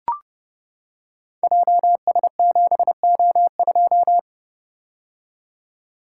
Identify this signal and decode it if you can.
Morse code, heard as a keyed tone, about 30 wpm, 700 Hz. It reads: JS7O2
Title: Code Groups - 5 Letter:Numbers - 30wpm